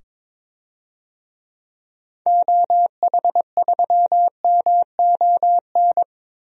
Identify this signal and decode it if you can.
Morse code, heard as a keyed tone, about 21 wpm, 700 Hz. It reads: OH3MON
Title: DXCC-100 Call Signs - 6 Characters - 22wpm